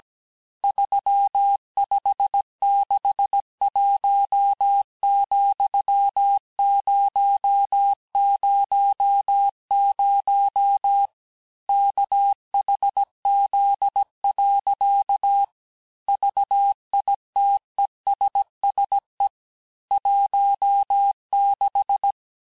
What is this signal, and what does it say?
Morse code, heard as a keyed tone, about 17 wpm, 800 Hz. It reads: 3561,000 KHZ. VITESSE 16
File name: cours-du-lundi files / cours-1587-texte-chiffre-vitesse-16MPM